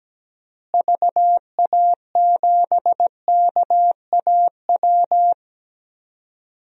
Morse code, heard as a keyed tone, about 17 wpm, 700 Hz. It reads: VA7KAW
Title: DXCC-100 Call Signs - 6 Characters - Repeated 2x - 17wpm